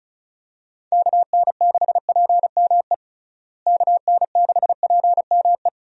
Morse code, heard as a keyed tone, about 36 wpm, 700 Hz. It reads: KN6PME KN6PME
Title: USA Call Signs - 6 Characters - Repeated 2x - 35wpm